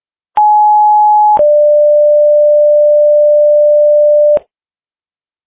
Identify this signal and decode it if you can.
Two-tone page, heard as a one-second tone, then a three-second tone, about 840 then 590 Hz. It is heard from 0.4 to 4.4 s.